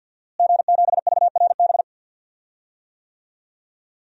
Morse code, heard as a keyed tone, about 52 wpm, 700 Hz. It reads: G7VRB
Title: DXCC-100 Call Signs - 5 Characters - Repeated 3x - 50wpm